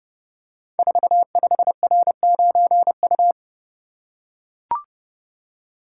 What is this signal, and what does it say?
Morse code, heard as a keyed tone, about 32 wpm, 700 Hz. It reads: 45R9U
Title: Code Groups - 5 Letter:Numbers - 30wpm